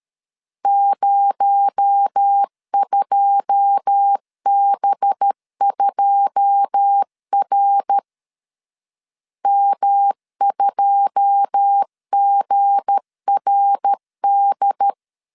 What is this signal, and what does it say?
Morse code, heard as a keyed tone, about 13 wpm, 800 Hz. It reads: 02B2R M2GRD
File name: LEZIONI / 9 2 12wpm 2301